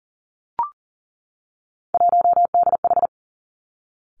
Morse code, heard as a keyed tone, about 38 wpm, 700 Hz. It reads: 1DH
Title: Three Challenging Letter-Number Combinations 40wpm